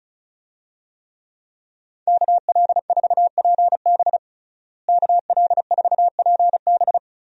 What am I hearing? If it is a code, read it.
Morse code, heard as a keyed tone, about 36 wpm, 700 Hz. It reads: KL4PB KL4PB
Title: USA Call Signs - 5 Characters - Repeated 3x - 35wpm